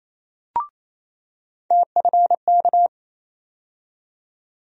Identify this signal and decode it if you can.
Morse code, heard as a keyed tone, about 27 wpm, 700 Hz. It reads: TFK
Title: USA Call Sign Postfixes - 3 Letters 28wpm